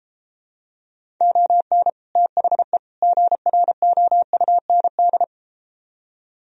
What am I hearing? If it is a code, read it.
Morse code, heard as a keyed tone, about 32 wpm, 700 Hz. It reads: ON THE GROUND